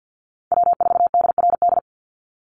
Morse code, heard as a keyed tone, about 49 wpm, 700 Hz. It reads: W4DRD
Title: USA Call Signs 50wpm